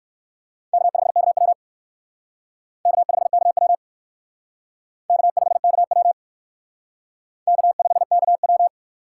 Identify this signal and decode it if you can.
Morse code, heard as a keyed tone, about 55 wpm, 700 Hz. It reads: K5KW K5KW K5KW K5KW
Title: Speed Racing - USA 1x2 Call Signs 45wpm